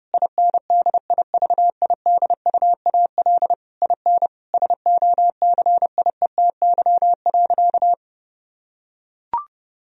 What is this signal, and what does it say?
Morse code, heard as a keyed tone, about 30 wpm, 700 Hz. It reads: INDIVIDUAL IN SOCIETY.
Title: Sentences from Top 1000 Words - Review Part 4 - 30wpm